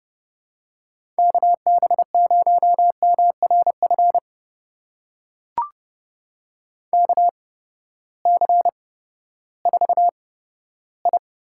Morse code, heard as a keyed tone, about 31 wpm, spaced as for 15 wpm, 700 Hz. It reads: KB0MRF KC4I
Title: USA Call Signs - 6 Characters - 8x Character Spacing - 30wpm